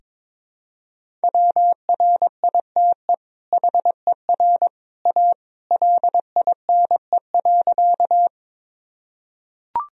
Morse code, heard as a keyed tone, about 22 wpm, 700 Hz. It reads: WRITE HER A LINE.
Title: Sentences from Top 1000 Words - Review Part 1 - 22wpm